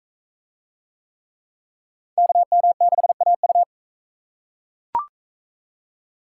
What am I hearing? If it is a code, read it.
Morse code, heard as a keyed tone, about 44 wpm, 700 Hz. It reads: KM6AU